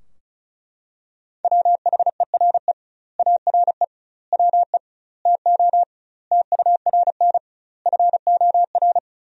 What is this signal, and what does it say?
Morse code, heard as a keyed tone, about 34 wpm, 700 Hz. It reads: WHERE ARE WE TO TURN FOR
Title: 2x Word-Spacing - Sentences from Top 200 Words - 35wpm